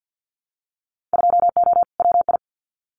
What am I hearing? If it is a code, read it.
Morse code, heard as a keyed tone, about 50 wpm, 700 Hz. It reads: 2O WI
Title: Winter Field Day Exchanges 50wpm